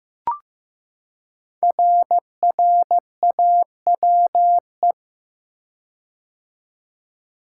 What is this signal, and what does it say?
Morse code, heard as a keyed tone, about 15 wpm, 700 Hz. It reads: RRAWE
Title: Code Groups - 5 Letters - 15wpm